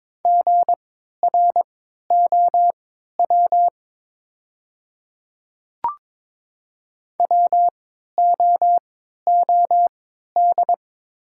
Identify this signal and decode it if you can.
Morse code, heard as a keyed tone, about 22 wpm, spaced as for 12 wpm, 700 Hz. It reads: GROW WOOD
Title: 3x Character Spacing - Top 300 Words - 22wpm